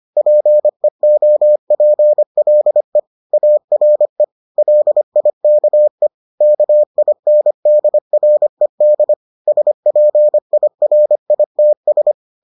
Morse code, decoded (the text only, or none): PEOPLE ARE LIKE KINDRED SPIRITS